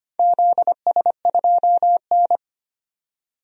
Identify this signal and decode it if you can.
Morse code, heard as a keyed tone, about 25 wpm, 700 Hz. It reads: ZS2N
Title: DXCC-100 Call Signs - 4 Characters - 25wpm